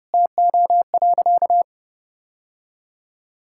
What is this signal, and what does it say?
Morse code, heard as a keyed tone, about 30 wpm, 700 Hz. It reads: TO.
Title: Sentences from Top 1000 Words - Review Part 6 - 30wpm